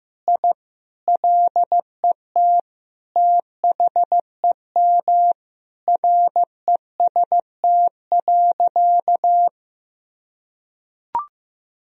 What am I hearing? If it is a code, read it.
Morse code, heard as a keyed tone, about 15 wpm, 700 Hz. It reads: I LET THEM REST.
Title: Sentences from Top 1000 Words - Review Part 4 - 15wpm